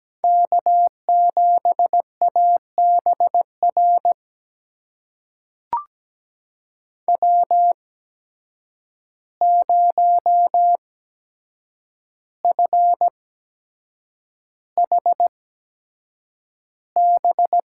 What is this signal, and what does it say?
Morse code, heard as a keyed tone, about 17 wpm, spaced as for 7 wpm, 700 Hz. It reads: K7ABR W0FHB
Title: USA Call Signs - 5 Characters - 8x Character Spacing - 17wpm